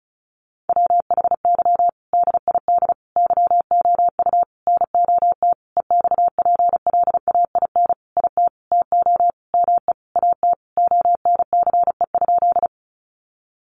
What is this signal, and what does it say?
Morse code, heard as a keyed tone, about 34 wpm, 700 Hz. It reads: WHY DID YOU NOT EXPLAIN IT TO ME AT ONCE?
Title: Sentences from Top 1100 Words 35wpm